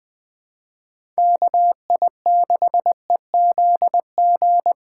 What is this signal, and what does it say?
Morse code, heard as a keyed tone, about 20 wpm, 700 Hz. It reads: KI6EZG
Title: USA Call Signs - 6 Characters - 3x Character Spacing - 20wpm